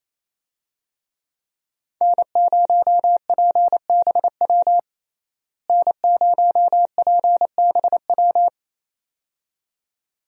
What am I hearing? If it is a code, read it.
Morse code, heard as a keyed tone, about 28 wpm, 700 Hz. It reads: N0PBW N0PBW
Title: USA Call Signs - 5 Characters - Repeated 2x - 28wpm